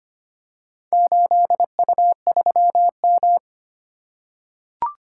Morse code, heard as a keyed tone, about 25 wpm, 700 Hz. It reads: 8U3M
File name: practice / 25wpm / Four Letter-Number Combinations - 25wpm